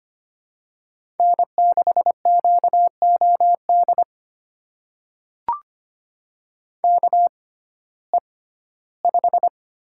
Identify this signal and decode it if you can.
Morse code, heard as a keyed tone, about 25 wpm, spaced as for 15 wpm, 700 Hz. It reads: N6QOD KE5
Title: USA Call Signs - 6x Character Spacing - 25wpm